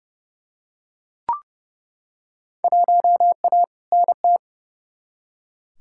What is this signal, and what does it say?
Morse code, heard as a keyed tone, about 29 wpm, 700 Hz. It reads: 1A NT
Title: Field Day Exchanges Encore - 30wpm